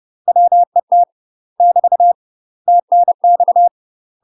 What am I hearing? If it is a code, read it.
Morse code, heard as a keyed tone, about 30 wpm, 700 Hz. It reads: WET <BT> TNX
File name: UK MASTER QSOs TRAINING / 30WPM / UK MASTER QSOs 0003